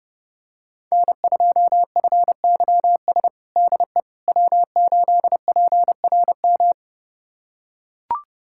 Morse code, heard as a keyed tone, about 30 wpm, 700 Hz. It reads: N2FYS DE W8PRM
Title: USA Callsign DE Callsign - 5 Characters - 30wpm